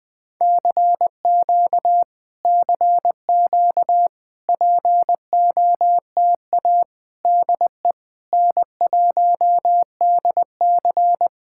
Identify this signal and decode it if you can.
Morse code, heard as a keyed tone, about 20 wpm, 700 Hz. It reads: CQ CQ POTA DE N1DC
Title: Speed-Racing Common QSO Phrases - 17wpm